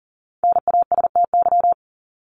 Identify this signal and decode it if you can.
Morse code, heard as a keyed tone, about 40 wpm, 700 Hz. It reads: NASTY
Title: Top 1751-2000 Words 40wpm